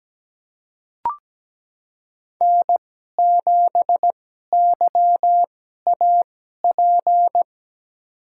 Morse code, heard as a keyed tone, about 17 wpm, spaced as for 12 wpm, 700 Hz. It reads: N7YAP
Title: USA Call Signs - 5 Characters - 2x Character Spacing - 17wpm